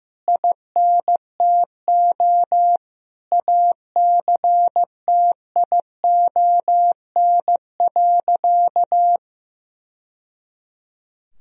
Morse code, heard as a keyed tone, about 15 wpm, 700 Hz. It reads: INTO ACTION.
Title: Sentences from Top 1000 Words - Review Part 4 - 15wpm